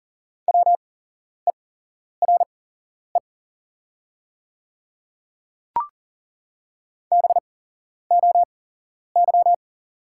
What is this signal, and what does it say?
Morse code, heard as a keyed tone, about 40 wpm, spaced as for 11 wpm, 700 Hz. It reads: WERE BOY
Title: 8x Character Spacing - Top 200 Words - 40wpm